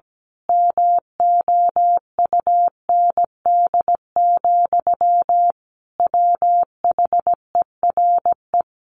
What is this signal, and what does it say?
Morse code, heard as a keyed tone, about 17 wpm, 700 Hz. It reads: MOUND, WHERE
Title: A Sound Of Thunder 17wpm